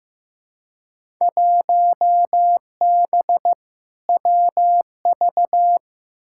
Morse code, heard as a keyed tone, about 15 wpm, 700 Hz. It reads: 1B WV
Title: Field Day Exchanges Encore - 15wpm